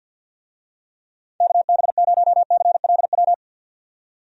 Morse code, heard as a keyed tone, about 48 wpm, 700 Hz. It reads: KB0CLW
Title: USA Call Signs - 6 Characters - 50wpm